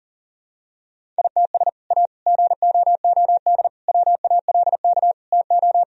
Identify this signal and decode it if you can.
Morse code, heard as a keyed tone, about 37 wpm, 700 Hz. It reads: ITS A GOOD WALK TO
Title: Sentences from Top 1000 Words - Review Part 1 - 40wpm